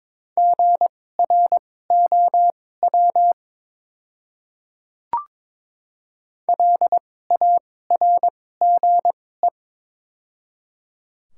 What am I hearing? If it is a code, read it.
Morse code, heard as a keyed tone, about 22 wpm, spaced as for 16 wpm, 700 Hz. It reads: GROW LARGE
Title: 2x Character Spacing - Top 300 Words - 22wpm